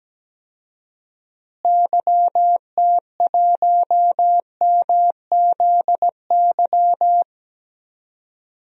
Morse code, heard as a keyed tone, about 17 wpm, 700 Hz. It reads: YT1MZY
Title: DXCC-100 Call Signs - 6 Characters - Repeated 2x - 17wpm